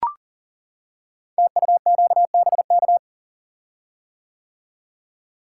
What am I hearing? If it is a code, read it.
Morse code, heard as a keyed tone, about 40 wpm, 700 Hz. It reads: TUQBK
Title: Code Groups - 5 Letters - 40wpm